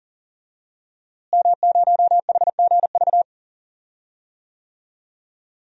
Morse code, heard as a keyed tone, about 40 wpm, 700 Hz. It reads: M0HGV
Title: DXCC-100 Call Signs - 5 Characters - Repeated 3x - 40wpm